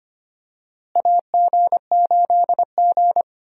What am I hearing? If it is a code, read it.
Morse code, heard as a keyed tone, about 25 wpm, 700 Hz. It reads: AG8G
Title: USA Call Signs - 4 Characters - Repeated 3x - 25wpm